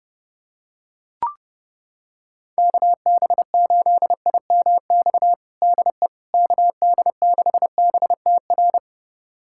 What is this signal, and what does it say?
Morse code, heard as a keyed tone, about 30 wpm, 700 Hz. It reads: KB8IMX DE KD6BTR
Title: USA Callsign DE Callsign 30wpm